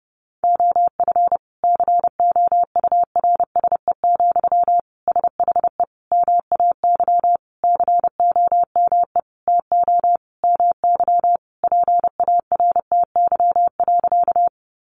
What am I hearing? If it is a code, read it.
Morse code, heard as a keyed tone, about 30 wpm, 700 Hz. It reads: OF COURSE, SHE MAY COME TO MY PARTY.
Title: Sentences from Top 1400 Words 30wpm